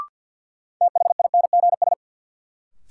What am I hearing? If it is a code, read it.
Morse code, heard as a keyed tone, about 51 wpm, 700 Hz. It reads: THINGS
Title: Sentences from Top 1400 Words - New Word Drill Alt - 50wpm